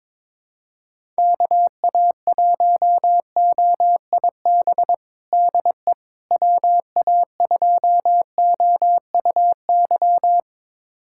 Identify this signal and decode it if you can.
Morse code, heard as a keyed tone, about 22 wpm, 700 Hz. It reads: KA1OIB DE WA2OUY
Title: USA Callsign DE Callsign 22wpm